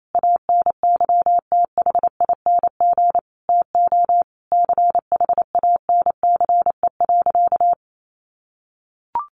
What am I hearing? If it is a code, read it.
Morse code, heard as a keyed tone, about 28 wpm, 700 Hz. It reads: ANYTHING TO CHANCE.
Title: Sentences from Top 1100 Words 28wpm